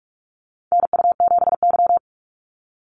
Morse code, heard as a keyed tone, about 44 wpm, 690 Hz. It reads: NU7Y
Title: NA SOTA Chaser Callsigns 45wpm